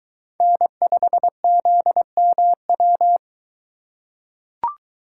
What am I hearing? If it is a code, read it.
Morse code, heard as a keyed tone, about 22 wpm, 700 Hz. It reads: N5ZMW